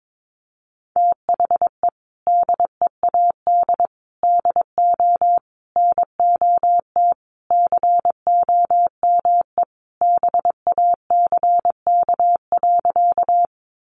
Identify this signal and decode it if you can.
Morse code, heard as a keyed tone, about 22 wpm, 700 Hz. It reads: THE DEAD DO NOT COME BACK.
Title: Sentences from Top 1500 Words 22wpm